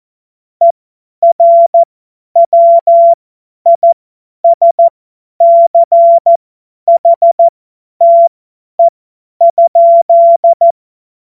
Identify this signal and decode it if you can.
Morse code, heard as a keyed tone, about 14 wpm, spaced as for 10 wpm, 680 Hz. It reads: ERWISCHTE?
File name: Morsekurs / Witze / Witz 01. Februar 2022 50 F70